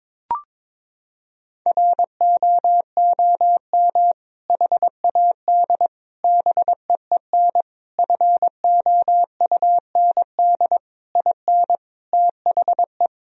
Morse code, heard as a keyed tone, about 22 wpm, 700 Hz. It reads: ROOM HAD BEEN FOUND IN THE
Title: Sentences from Top 1000 Words - Review Part 3 - 22wpm